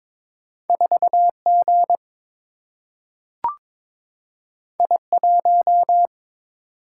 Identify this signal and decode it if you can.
Morse code, heard as a keyed tone, about 22 wpm, 700 Hz. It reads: 4 G I 1
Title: Single-Letter Single-Number 22wpm